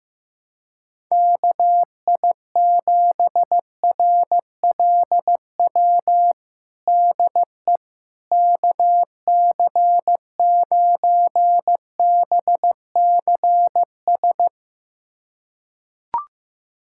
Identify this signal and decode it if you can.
Morse code, heard as a keyed tone, about 16 wpm, 700 Hz. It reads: KI7RLW DE KC9BCS
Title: USA Callsign DE Callsign - 6 Characters - 15wpm